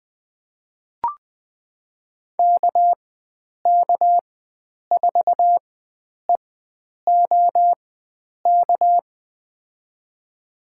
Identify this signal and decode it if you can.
Morse code, heard as a keyed tone, about 21 wpm, spaced as for 9 wpm, 700 Hz. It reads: KK4EOK
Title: USA Call Signs - 4x Character Spacing - 20wpm